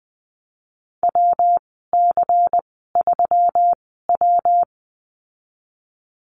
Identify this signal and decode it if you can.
Morse code, heard as a keyed tone, about 20 wpm, spaced as for 15 wpm, 700 Hz. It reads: WC3W
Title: 122 Lesson 20g - Number 3 - Call Signs 20wpm